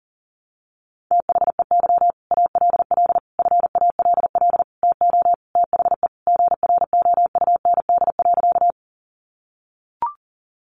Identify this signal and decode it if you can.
Morse code, heard as a keyed tone, about 40 wpm, 700 Hz. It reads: THEY ALL FALL TO THE GROUND.
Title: Sentences from Top 500 Words 40wpm